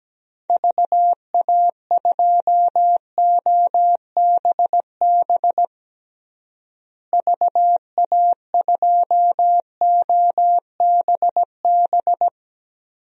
Morse code, heard as a keyed tone, about 17 wpm, 700 Hz. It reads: VA2OBB VA2OBB
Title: DXCC-100 Call Signs - 6 Characters - Repeated 3x - 17wpm